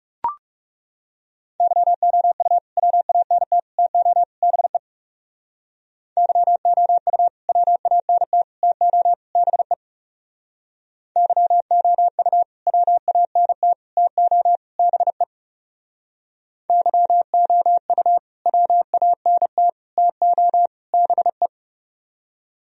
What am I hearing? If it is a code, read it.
Morse code, heard as a keyed tone, about 37 wpm, 700 Hz. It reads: YOU WANT TO BE YOU WANT TO BE YOU WANT TO BE YOU WANT TO BE
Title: Speed-Racing - Sets of 4 Words 30wpm